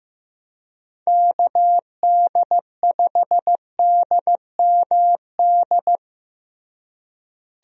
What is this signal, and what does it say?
Morse code, heard as a keyed tone, about 16 wpm, 700 Hz. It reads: KD5DMD
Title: USA Call Signs - 6 Characters - Repeated 2x - 15wpm